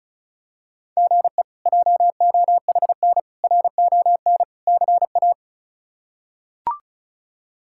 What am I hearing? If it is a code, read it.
Morse code, heard as a keyed tone, about 36 wpm, 700 Hz. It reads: GE JOHN RON CA
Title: K1USN SST QSO Practice - 35wpm